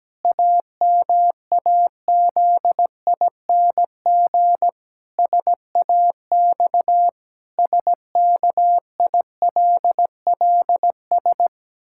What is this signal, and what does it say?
Morse code, heard as a keyed tone, about 17 wpm, 700 Hz. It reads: AMAZING SAX SKILLS
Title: Pangram Challenge v2 - 17wpm